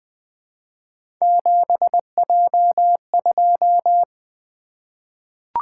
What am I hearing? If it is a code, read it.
Morse code, heard as a keyed tone, about 20 wpm, 700 Hz. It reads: 7J2
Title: Three Letter-Number Combinations - 20wpm